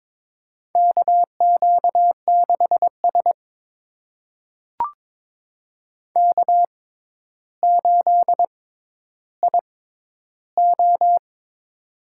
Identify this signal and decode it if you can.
Morse code, heard as a keyed tone, about 22 wpm, spaced as for 11 wpm, 700 Hz. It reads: KQ6S K8IO
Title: USA Call Signs - 4 Characters - 6x Character Spacing - 22wpm